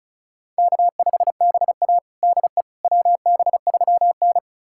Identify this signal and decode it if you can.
Morse code, heard as a keyed tone, about 34 wpm, 700 Hz. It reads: K5BA DE WB3N
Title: USA Callsign DE Callsign - 4 Characters - 35wpm